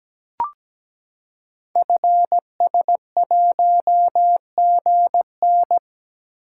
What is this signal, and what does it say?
Morse code, heard as a keyed tone, about 17 wpm, 700 Hz. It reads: FS1GN
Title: Code Groups - 5 Letter:Numbers - 17wpm